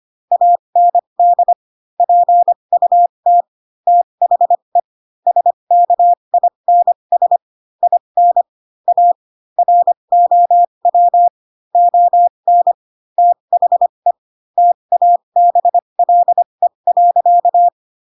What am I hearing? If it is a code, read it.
Morse code, heard as a keyed tone, about 25 wpm, spaced as for 22 wpm, 700 Hz. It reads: AND PUT THE SKINS IN A ROW ON THE TABLE.
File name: Pinocchio / Pinocchio ch7 18 25 ik1hge